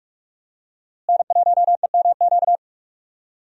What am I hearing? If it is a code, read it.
Morse code, heard as a keyed tone, about 46 wpm, 700 Hz. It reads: N1EMQ
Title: USA Call Signs - 5 Characters - Repeated 2x - 45wpm